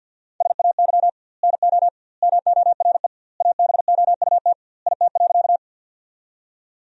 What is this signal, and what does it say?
Morse code, heard as a keyed tone, about 50 wpm, 700 Hz. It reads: SAY NO MORE ABOUT IT.
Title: 2x Word-Spacing - Sentences from Top 200 Words - 50wpm